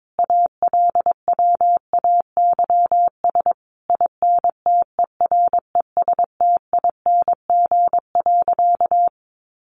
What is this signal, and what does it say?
Morse code, heard as a keyed tone, about 22 wpm, 700 Hz. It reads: ALWAYS INTERESTING.
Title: Sentences from Top 1200 Words 22wpm